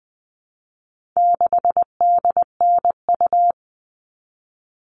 Morse code, heard as a keyed tone, about 20 wpm, 700 Hz. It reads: 6DNU
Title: All Cut Numbers - 4-Digit 20wpm